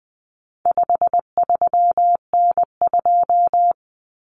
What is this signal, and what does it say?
Morse code, heard as a keyed tone, about 20 wpm, 700 Hz. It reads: 53N2
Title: Common Cut Numbers - 4-Digit 20wpm